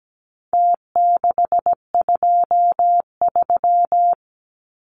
Morse code, heard as a keyed tone, about 17 wpm, 700 Hz. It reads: T623
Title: Numbers - Contest Style - 4-Digit 17wpm